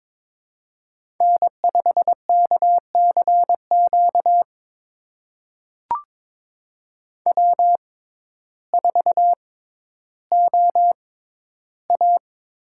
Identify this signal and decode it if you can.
Morse code, heard as a keyed tone, about 23 wpm, spaced as for 12 wpm, 700 Hz. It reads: N5KCQ W4OA
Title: USA Call Signs - 6x Character Spacing - 22wpm